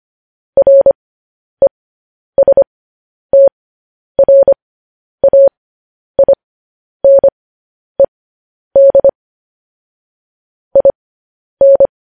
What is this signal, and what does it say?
Morse code, heard as a keyed tone, about 25 wpm, spaced as for 10 wpm, 550 Hz. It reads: RESTRAINED IN
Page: news headlines in morse code at 10 wpm